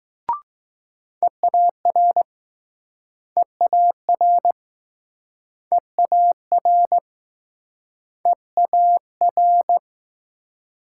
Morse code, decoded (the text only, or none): EAR EAR EAR EAR